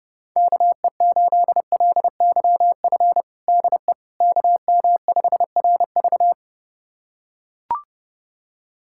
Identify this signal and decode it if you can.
Morse code, heard as a keyed tone, about 30 wpm, 700 Hz. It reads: KE8LYF DE KM5RV
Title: USA Callsign DE Callsign 30wpm